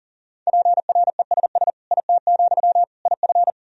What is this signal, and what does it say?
Morse code, heard as a keyed tone, about 40 wpm, 700 Hz. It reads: PRESS IT, IF
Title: Sentences from Top 1000 Words - Review Part 1 - 40wpm